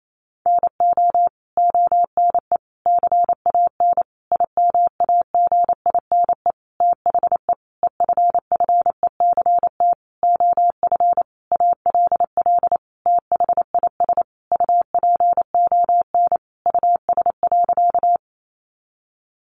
Morse code, heard as a keyed tone, about 28 wpm, 700 Hz. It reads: NO ONE CAN IMAGINE THE EFFECT OF ALL THIS UPON US.
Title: Sentences from Top 1400 Words 28wpm